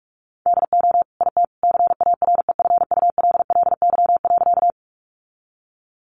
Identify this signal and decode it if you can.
Morse code, heard as a keyed tone, about 46 wpm, 700 Hz. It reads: DO IT CAREFULLY.
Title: Sentences from Top 1300 Words 45wpm